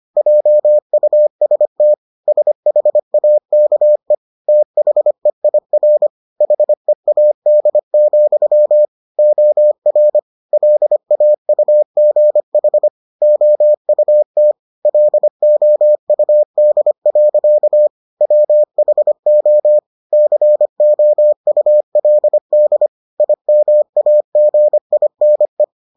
Morse code, heard as a keyed tone, about 25 wpm, 600 Hz. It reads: JUST SHAKE THEIR HEAD, OR LAUGH OUT LOUD. WHO COULD IMAGINE